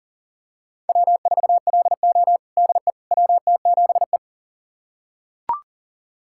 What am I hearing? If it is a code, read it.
Morse code, heard as a keyed tone, about 37 wpm, 700 Hz. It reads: W4LO DE WT7E
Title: USA Callsign DE Callsign - 4 Characters - 40wpm